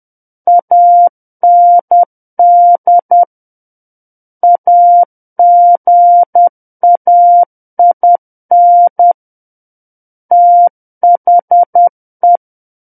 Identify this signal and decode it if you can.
Morse code, heard as a keyed tone, about 10 wpm, 700 Hz. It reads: AND AGAIN THE
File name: Book One / 01-Chapter 1 - 10 WPM